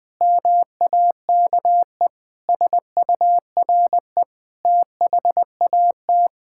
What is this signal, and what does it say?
Morse code, heard as a keyed tone, about 20 wpm, 700 Hz. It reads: MAKE SURE THAT
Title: Sentences with 6-Word Phrases - 20wpm